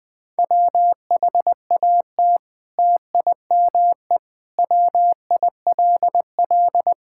Morse code, heard as a keyed tone, about 20 wpm, 700 Hz. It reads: WHAT TIME WILL
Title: Sentences from Top 1000 Words - Review Part 5 - 20wpm